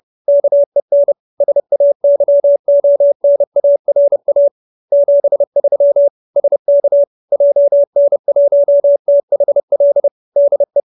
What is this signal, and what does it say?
Morse code, heard as a keyed tone, about 30 wpm, 570 Hz. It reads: KEN SAYONARA 73 SK JN1THL DE
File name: QSO309 30